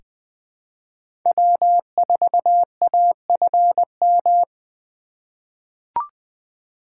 Morse code, heard as a keyed tone, about 20 wpm, 700 Hz. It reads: W4AFM
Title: USA Call Signs - 3x Character Spacing - 20wpm